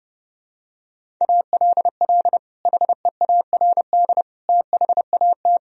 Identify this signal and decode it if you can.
Morse code, heard as a keyed tone, about 32 wpm, 700 Hz. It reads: ALL HEARD THAT
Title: Sentences from Top 1000 Words - Review Part 1 - 30wpm